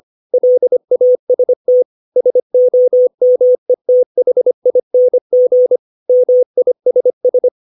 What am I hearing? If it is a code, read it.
Morse code, heard as a keyed tone, about 25 wpm, 500 Hz. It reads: LAST SOMETHING MISS
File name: 632 words / 632 Words 25WPM